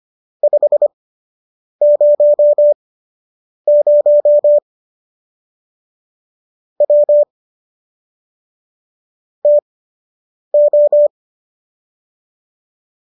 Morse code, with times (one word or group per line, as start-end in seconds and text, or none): 0.4-4.6 500
6.8-7.2 W
9.5-11.1 TO